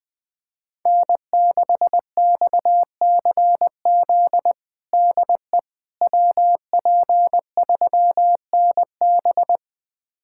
Morse code, heard as a keyed tone, about 20 wpm, 700 Hz. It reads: N6XCZ DE WP3NB
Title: USA Callsign DE Callsign - 5 Characters - 20wpm